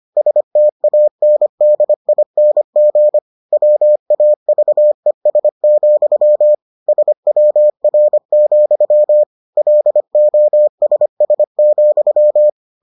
Morse code, heard as a keyed tone, about 25 wpm, 600 Hz. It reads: STANDING WAVES, SWR, LOSS,